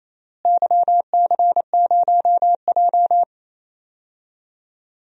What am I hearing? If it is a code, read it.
Morse code, heard as a keyed tone, about 28 wpm, 700 Hz. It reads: YC0J
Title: DXCC-100 Call Signs - 4 Characters - Repeated 3x - 28wpm